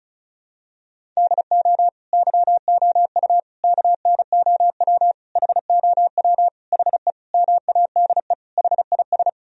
Morse code, heard as a keyed tone, about 34 wpm, 700 Hz. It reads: DO YOU KNOW HOW HE MADE HIS
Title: Sentences from Top 1000 Words - Review Part 1 - 35wpm